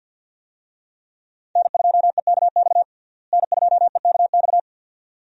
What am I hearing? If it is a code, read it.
Morse code, heard as a keyed tone, about 53 wpm, 700 Hz. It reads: N2EKX N2EKX
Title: USA Call Signs - 5 Characters - Repeated 2x - 50wpm